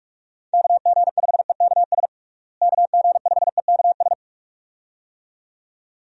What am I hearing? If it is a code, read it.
Morse code, heard as a keyed tone, about 45 wpm, 700 Hz. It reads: KG5EKS KG5EKS
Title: USA Call Signs - 6 Characters - Repeated 2x - 45wpm